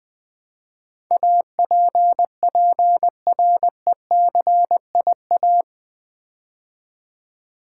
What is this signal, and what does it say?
Morse code, heard as a keyed tone, about 20 wpm, 700 Hz. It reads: APPRECIA
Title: Build A Word - 4 Letters At A Time - 12-Letter Words - 20wpm